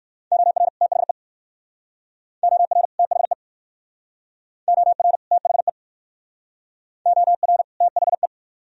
Morse code, heard as a keyed tone, about 54 wpm, 700 Hz. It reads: OR THE OR THE OR THE OR THE
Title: Speed-Racing - Sets of 2 Words 45wpm